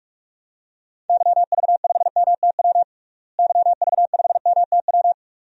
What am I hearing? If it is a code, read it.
Morse code, heard as a keyed tone, about 43 wpm, 700 Hz. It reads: YV5MTW YV5MTW
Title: DXCC-100 Call Signs - 6 Characters - Repeated 2x - 45wpm